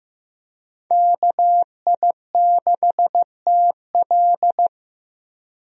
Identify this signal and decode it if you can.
Morse code, heard as a keyed tone, about 15 wpm, 700 Hz. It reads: KI6TL
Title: USA Call Signs - 5 Characters - 15wpm